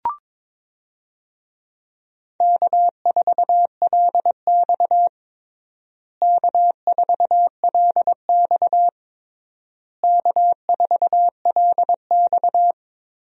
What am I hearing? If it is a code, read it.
Morse code, heard as a keyed tone, about 22 wpm, 700 Hz. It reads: K4LX K4LX K4LX
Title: USA Call Signs - 4 Characters - Repeated 3x - 22wpm